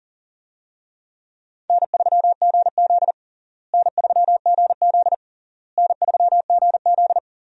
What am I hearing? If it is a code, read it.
Morse code, heard as a keyed tone, about 37 wpm, 700 Hz. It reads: N3GZ N3GZ N3GZ
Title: USA Call Signs - 4 Characters - Repeated 3x - 40wpm